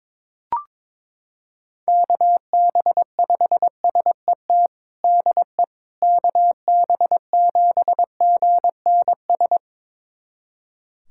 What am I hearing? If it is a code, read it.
Morse code, heard as a keyed tone, about 22 wpm, 700 Hz. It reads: KB5SET DE KB7GNS
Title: USA Callsign DE Callsign - 6 Characters - 22wpm